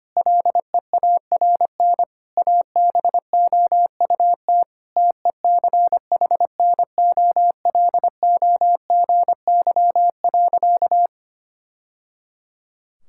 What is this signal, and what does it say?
Morse code, heard as a keyed tone, about 25 wpm, 700 Hz. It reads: LEARN ABOUT TECHNOLOGY.
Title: Sentences from Top 1000 Words - Review Part 5 - 25wpm